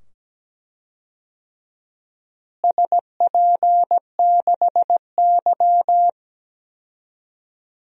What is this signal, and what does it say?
Morse code, heard as a keyed tone, about 17 wpm, 700 Hz. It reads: SP6Y
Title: DXCC-100 Call Signs - 4 Characters - Repeated 2x - 17wpm